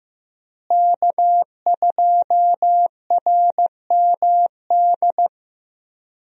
Morse code, heard as a keyed tone, about 15 wpm, 700 Hz. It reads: K2RMD
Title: USA Call Signs - 5 Characters - Repeated 2x - 15wpm